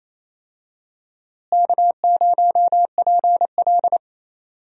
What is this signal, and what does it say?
Morse code, heard as a keyed tone, about 28 wpm, 700 Hz. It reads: K0PL
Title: USA Call Signs - 4 Characters - Repeated 3x - 28wpm